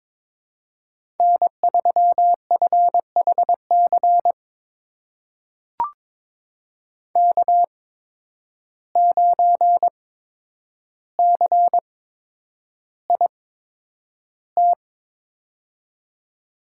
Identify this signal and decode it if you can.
Morse code, heard as a keyed tone, about 22 wpm, spaced as for 9 wpm, 700 Hz. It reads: N3FHC K9CIT
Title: USA Call Signs - 5 Characters - 8x Character Spacing - 22wpm